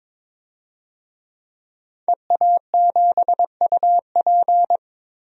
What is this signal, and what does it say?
Morse code, heard as a keyed tone, about 22 wpm, 700 Hz. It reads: EA7UP
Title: DXCC-100 Call Signs - 5 Characters - 22wpm